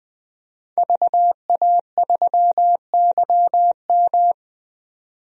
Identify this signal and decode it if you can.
Morse code, heard as a keyed tone, about 20 wpm, 700 Hz. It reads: VA3YM
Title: DXCC-100 Call Signs - 5 Characters - 20wpm